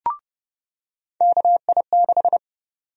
Morse code, heard as a keyed tone, about 30 wpm, 700 Hz. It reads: KI6
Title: USA Call Sign Prefixes - 30wpm